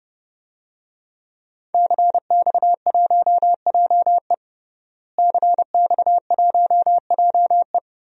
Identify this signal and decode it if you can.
Morse code, heard as a keyed tone, about 29 wpm, 700 Hz. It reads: CX1JE CX1JE
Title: DXCC-100 Call Signs - 5 Characters - Repeated 3x - 30wpm